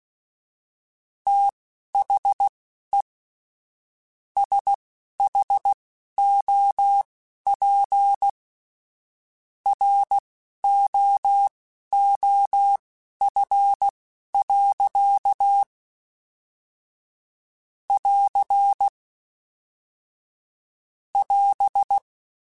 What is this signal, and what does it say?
Morse code, heard as a keyed tone, about 16 wpm, spaced as for 11 wpm, 800 Hz. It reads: THE SHOP ROOF. <AR> <AS>
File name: Weather Sets / Weather Sets-mp3s / Weather 4 28 @10